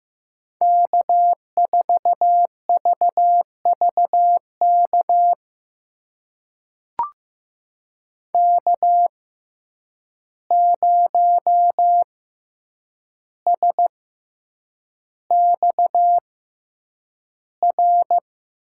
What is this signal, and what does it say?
Morse code, heard as a keyed tone, about 15 wpm, spaced as for 8 wpm, 700 Hz. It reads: K4VVK K0SXR
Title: USA Call Signs - 5 Characters - 6x Character Spacing - 15wpm